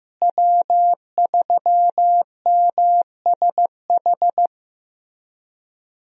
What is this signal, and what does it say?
Morse code, heard as a keyed tone, about 15 wpm, 700 Hz. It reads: W3MSH
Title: USA Call Signs - 5 Characters - Repeated 3x - 15wpm